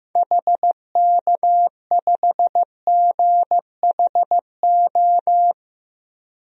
Morse code, heard as a keyed tone, about 15 wpm, 700 Hz. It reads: HK5GHO
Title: DXCC-100 Call Signs - 6 Characters - 15wpm